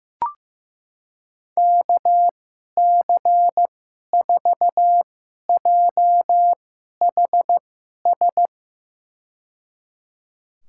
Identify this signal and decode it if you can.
Morse code, heard as a keyed tone, about 15 wpm, spaced as for 11 wpm, 700 Hz. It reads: KC4JHS
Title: USA Call Signs - 6 Characters - 2x Character Spacing - 15wpm